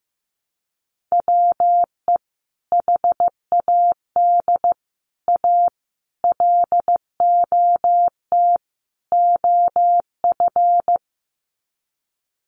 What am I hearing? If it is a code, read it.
Morse code, heard as a keyed tone, about 15 wpm, 700 Hz. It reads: WE HAD A LOT OF
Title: Sets of 5 Words Encore 15wpm